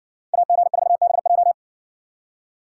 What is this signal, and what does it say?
Morse code, heard as a keyed tone, about 60 wpm, 700 Hz. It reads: AG4BJ